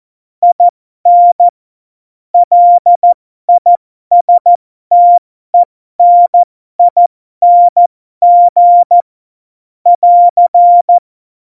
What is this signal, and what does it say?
Morse code, heard as a keyed tone, about 14 wpm, spaced as for 12 wpm, 700 Hz. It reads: IN LISTENING <AR>
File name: Morsekurs / Witze